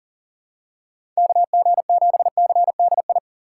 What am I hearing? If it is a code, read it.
Morse code, heard as a keyed tone, about 40 wpm, 690 Hz. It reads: KG7CDI
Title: USA Call Signs - 6 Characters - 2x Character Spacing - 40wpm